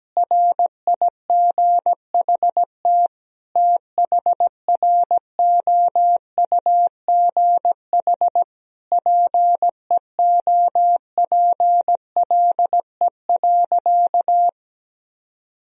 Morse code, heard as a keyed tone, about 17 wpm, 700 Hz. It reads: RIGHT THROUGH PEOPLE.
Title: Sentences from Top 200 Words and Abbreviations 17wpm